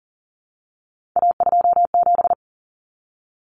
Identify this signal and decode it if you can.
Morse code, heard as a keyed tone, about 40 wpm, 700 Hz. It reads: A27
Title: Common Cut Numbers - 3-Digit 40wpm